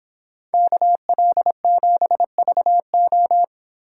Morse code, heard as a keyed tone, about 25 wpm, 700 Hz. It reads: KL7VO